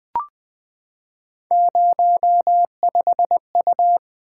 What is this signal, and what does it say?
Morse code, heard as a keyed tone, about 20 wpm, 700 Hz. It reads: 05U
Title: Three Letter-Number Combinations - 20wpm